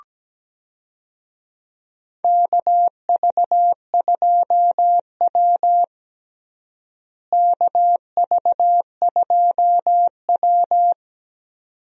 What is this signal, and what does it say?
Morse code, heard as a keyed tone, about 17 wpm, 700 Hz. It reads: KV2W KV2W
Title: USA Call Signs - 4 Characters - Repeated 2x - 17wpm